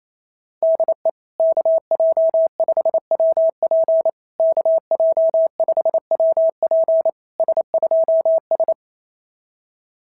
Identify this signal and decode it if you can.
Morse code, heard as a keyed tone, about 27 wpm, 650 Hz. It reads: DE KJ5WP KJ5WP S2S
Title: SOTA QSO Practice 28wpm